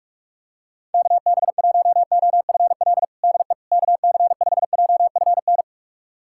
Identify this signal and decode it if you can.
Morse code, heard as a keyed tone, about 46 wpm, 700 Hz. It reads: KB1OFL DE KC5JFN
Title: USA Callsign DE Callsign - 6 Characters - 45wpm